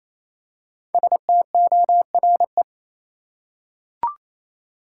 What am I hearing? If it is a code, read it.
Morse code, heard as a keyed tone, about 28 wpm, 700 Hz. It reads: STORE
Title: Five Letter Words - 28wpm